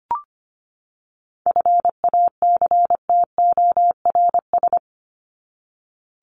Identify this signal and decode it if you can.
Morse code, heard as a keyed tone, about 25 wpm, 700 Hz. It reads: FACTORS
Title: ES:S Suffix - Top 100 Words with Base Words 25wpm